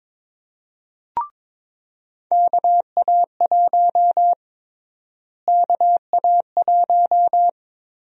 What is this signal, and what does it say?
Morse code, heard as a keyed tone, about 22 wpm, 700 Hz. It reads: KA1 KA1
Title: USA Call Sign Prefixes - Repeated 2x - 22wpm